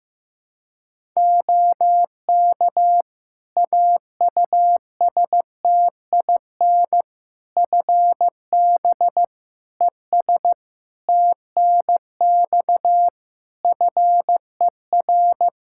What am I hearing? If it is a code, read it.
Morse code, heard as a keyed tone, about 15 wpm, 700 Hz. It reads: OK AUSTIN FB ES TNX FER
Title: Practice QSO - Exchange 2 - Info - 15wpm